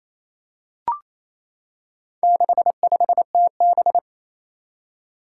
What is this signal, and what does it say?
Morse code, heard as a keyed tone, about 28 wpm, 700 Hz. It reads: 65TB
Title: Four Challenging Letter-Number Combinations 28wpm